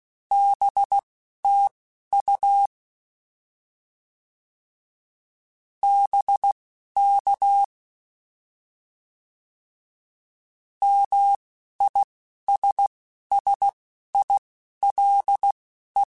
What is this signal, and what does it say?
Morse code, heard as a keyed tone, about 16 wpm, spaced as for 11 wpm, 800 Hz. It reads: BTU BK MISSILE
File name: Short Runs / Short Runs-mp3s / Words @10